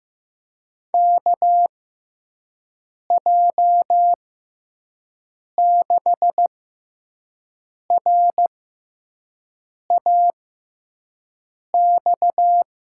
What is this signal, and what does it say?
Morse code, heard as a keyed tone, about 15 wpm, spaced as for 5 wpm, 700 Hz. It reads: KJ6RAX
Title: USA Call Signs - 6 Characters - 6x Character Spacing - 15wpm